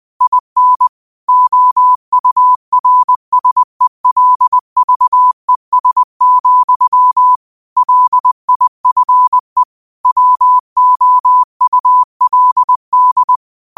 Morse code, heard as a keyed tone, about 20 wpm, 1000 Hz. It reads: IN OURSELVES, LIFE WOULD